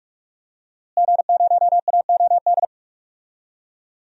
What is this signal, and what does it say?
Morse code, heard as a keyed tone, about 46 wpm, 700 Hz. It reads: G0AOD
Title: DXCC-100 Call Signs - 5 Characters - Repeated 2x - 45wpm